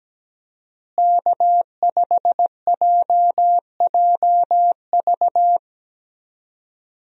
Morse code, heard as a keyed tone, about 17 wpm, 700 Hz. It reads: K5JJV
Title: USA Call Signs - 5 Characters - Repeated 2x - 17wpm